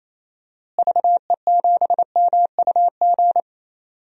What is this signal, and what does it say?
Morse code, heard as a keyed tone, about 27 wpm, 700 Hz. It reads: VE7MUG
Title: DXCC-100 Call Signs - 6 Characters - Repeated 3x - 28wpm